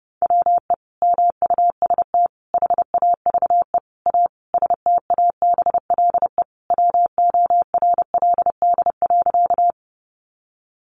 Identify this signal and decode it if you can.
Morse code, heard as a keyed tone, about 30 wpm, 700 Hz. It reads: WE MUST HAVE A STABLE WORLD.
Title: Sentences from Top 1500 Words 30wpm